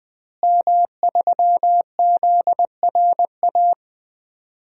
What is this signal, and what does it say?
Morse code, heard as a keyed tone, about 20 wpm, 700 Hz. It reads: M3ZRA
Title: DXCC-100 Call Signs - 5 Characters - Repeated 2x - 20wpm